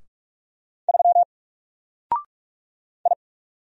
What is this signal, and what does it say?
Morse code, heard as a keyed tone, about 44 wpm, 700 Hz. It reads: 3 I
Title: Single Character (Letter:Number:Punctuation:Prosign) 45wpm